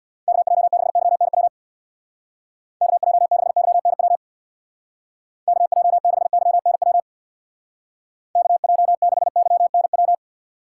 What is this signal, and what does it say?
Morse code, heard as a keyed tone, about 65 wpm, 700 Hz. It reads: KJ6YNW KJ6YNW KJ6YNW KJ6YNW